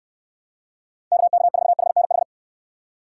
Morse code, heard as a keyed tone, about 67 wpm, 700 Hz. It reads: KK4HNH